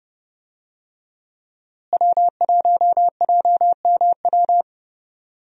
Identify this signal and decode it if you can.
Morse code, heard as a keyed tone, about 30 wpm, 700 Hz. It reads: W1JMW
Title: USA Call Signs - 5 Characters - Repeated 3x - 30wpm